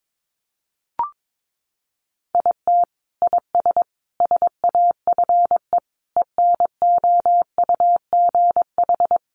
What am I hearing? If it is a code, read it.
Morse code, heard as a keyed tone, about 22 wpm, 700 Hz. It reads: IT IS SAFE ENOUGH
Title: Sentences from Top 900 Words 22wpm